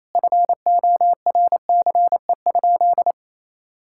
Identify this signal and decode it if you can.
Morse code, heard as a keyed tone, about 28 wpm, 700 Hz. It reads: FORCE?
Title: Sentences from Top 1600 Words 28wpm